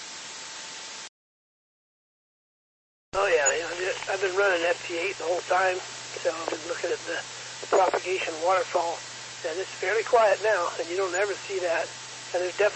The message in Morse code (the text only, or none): none